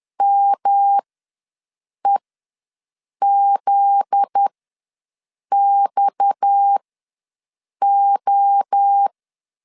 Morse code, heard as a keyed tone, about 10 wpm, spaced as for 6 wpm, 800 Hz. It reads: MEZXO
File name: LEZIONI / 7 GF9Z / 7 GF9Z ws 10wpm 3919